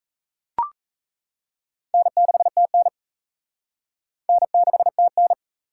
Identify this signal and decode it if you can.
Morse code, heard as a keyed tone, about 35 wpm, 700 Hz. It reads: N6TN N6TN